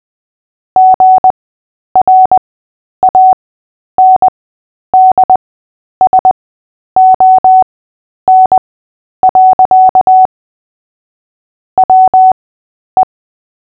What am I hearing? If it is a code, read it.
Morse code, heard as a keyed tone, about 20 wpm, spaced as for 10 wpm, 750 Hz. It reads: GRANDSON. WE
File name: Short-Story-5-10wpm